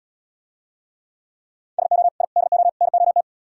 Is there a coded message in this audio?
Yes: Morse code from 1.8 to 3.2 s, heard as a keyed tone, about 77 wpm, 700 Hz. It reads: SO I DO NOT